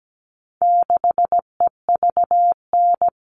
Morse code, heard as a keyed tone, about 17 wpm, 700 Hz. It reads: 6EVN
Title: All Cut Numbers - 4-Digit 17wpm